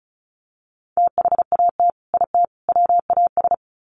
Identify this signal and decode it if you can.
Morse code, heard as a keyed tone, about 34 wpm, 700 Hz. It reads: THAT IT WAS
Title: Sets of 3 Words 35wpm